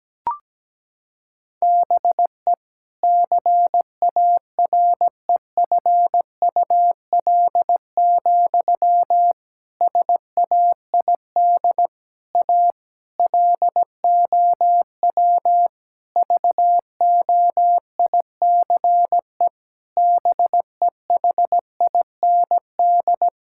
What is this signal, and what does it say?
Morse code, heard as a keyed tone, about 17 wpm, 700 Hz. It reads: BE CAREFUL, SAID A LOW VOICE BEHIND
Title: Sentences from Top 1600 Words 17wpm